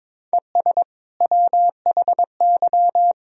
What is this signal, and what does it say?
Morse code, heard as a keyed tone, about 21 wpm, 700 Hz. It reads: ES WHY
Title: Sentences from Top 300 Words and Abbreviations 22wpm